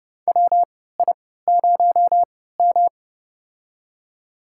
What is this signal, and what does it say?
Morse code, heard as a keyed tone, about 30 wpm, spaced as for 17 wpm, 700 Hz. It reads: WI0M
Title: USA Call Signs - 4 Characters - 3x Character Spacing - 30wpm